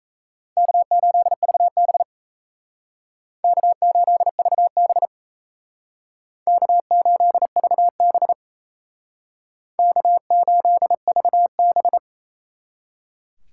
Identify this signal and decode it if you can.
Morse code, heard as a keyed tone, about 37 wpm, 700 Hz. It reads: K8VB K8VB K8VB K8VB